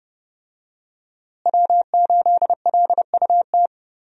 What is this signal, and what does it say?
Morse code, heard as a keyed tone, about 30 wpm, 700 Hz. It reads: W8LUT
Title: USA Call Signs - 5 Characters - Repeated 3x - 30wpm